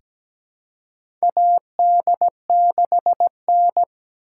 Morse code, heard as a keyed tone, about 17 wpm, 700 Hz. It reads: AD6N
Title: USA Call Signs - 4 Characters - 3x Character Spacing - 17wpm